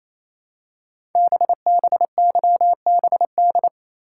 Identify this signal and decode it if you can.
Morse code, heard as a keyed tone, about 27 wpm, 700 Hz. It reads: BBYBD
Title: Code Groups - 5 Letters - 28wpm